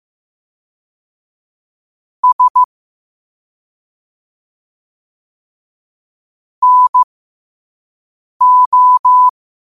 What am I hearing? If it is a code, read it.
Morse code, heard as a keyed tone, about 15 wpm, spaced as for 5 wpm, 1000 Hz. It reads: S NO
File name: qotdfiles / 5wpm